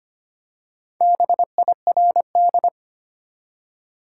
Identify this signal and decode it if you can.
Morse code, heard as a keyed tone, about 25 wpm, 700 Hz. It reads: BIRD
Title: Four Letter Words - 25wpm